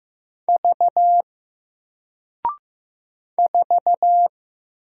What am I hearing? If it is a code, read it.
Morse code, heard as a keyed tone, about 15 wpm, 700 Hz. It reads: V 4
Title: Single Character (Letter:Number:Punctuation:Prosign) Alt 15wpm